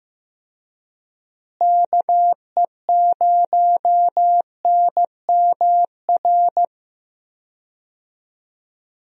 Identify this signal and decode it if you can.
Morse code, heard as a keyed tone, about 15 wpm, 700 Hz. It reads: KE0NMR
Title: USA Call Signs - 6 Characters - Repeated 2x - 15wpm